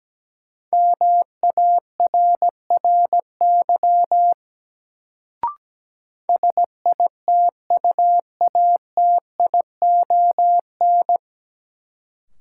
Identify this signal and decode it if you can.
Morse code, heard as a keyed tone, about 17 wpm, 700 Hz. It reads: MARRY SITUATION
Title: Sentences from Top 1200 Words - New Word Drill - 17wpm